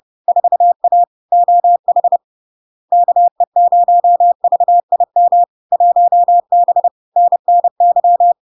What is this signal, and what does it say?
Morse code, heard as a keyed tone, about 30 wpm, 700 Hz. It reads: 4A OH KE0VIM 1B NNY
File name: LARK-FD-2022-Log-in-CW-30wpm-700-Hz